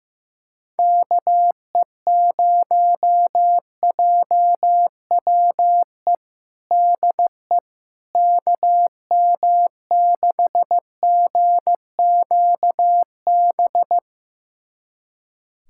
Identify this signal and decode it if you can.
Morse code, heard as a keyed tone, about 15 wpm, 700 Hz. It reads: KE0JWE DE KM6GQB
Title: USA Callsign DE Callsign 15wpm